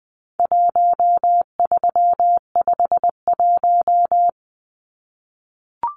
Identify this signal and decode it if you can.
Morse code, heard as a keyed tone, about 20 wpm, 700 Hz. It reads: 1351
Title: Numbers - 4-Digit 20wpm